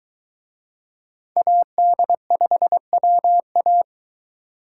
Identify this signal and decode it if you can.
Morse code, heard as a keyed tone, about 21 wpm, 700 Hz. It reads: AD5WA